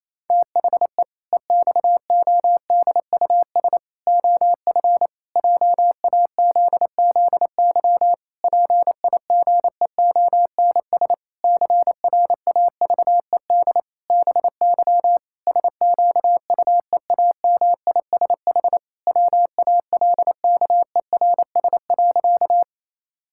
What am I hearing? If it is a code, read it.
Morse code, heard as a keyed tone, about 27 wpm, 700 Hz. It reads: THE EXODUS OF JAZZY PIGEONS CRAVED BY SQUEAMISH WALKERS.
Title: Pangram Challenge v2 - 28wpm